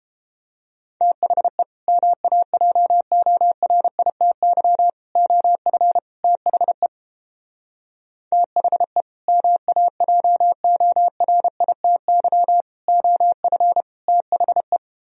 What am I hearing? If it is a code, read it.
Morse code, heard as a keyed tone, about 31 wpm, 700 Hz. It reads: THE MAJORITY OF THE THE MAJORITY OF THE